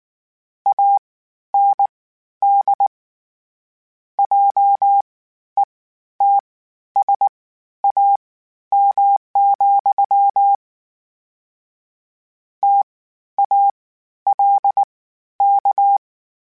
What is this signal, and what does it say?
Morse code, heard as a keyed tone, about 19 wpm, spaced as for 11 wpm, 800 Hz. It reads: AND JETSAM, TALK
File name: Misc / Misc-mp3s / WordPairs sp @18